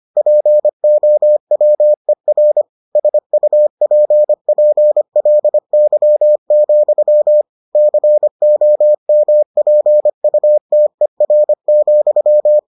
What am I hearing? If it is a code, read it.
Morse code, heard as a keyed tone, about 25 wpm, 600 Hz. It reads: POWER SUPPLY, COMPUTER,